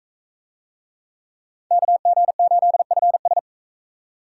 Morse code, heard as a keyed tone, about 42 wpm, 700 Hz. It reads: KG8FS